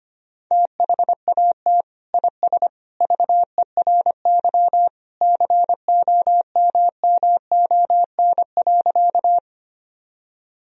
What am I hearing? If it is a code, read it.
Morse code, heard as a keyed tone, about 25 wpm, 700 Hz. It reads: THAT IS VERY COMMON.
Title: Sentences from Top 1000 Words - Review Part 3 - 25wpm